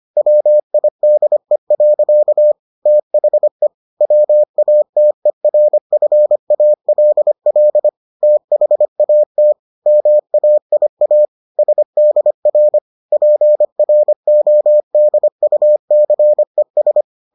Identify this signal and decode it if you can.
Morse code, heard as a keyed tone, about 25 wpm, 600 Hz. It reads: WIDE. THE WATERFALL THAT MAIA SDR PRODUCES